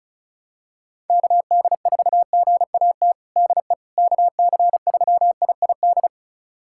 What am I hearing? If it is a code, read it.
Morse code, heard as a keyed tone, about 36 wpm, 700 Hz. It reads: KD4GAT DE KC3IID
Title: USA Callsign DE Callsign - 6 Characters - 35wpm